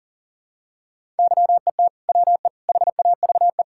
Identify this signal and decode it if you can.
Morse code, heard as a keyed tone, about 40 wpm, 700 Hz. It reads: YET WE HAVE
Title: Sentences from Top 1000 Words - Review Part 5 - 40wpm